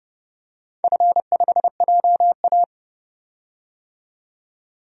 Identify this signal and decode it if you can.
Morse code, heard as a keyed tone, about 30 wpm, 700 Hz. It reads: F5JA
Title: DXCC-100 Call Signs - 4 Characters - 30wpm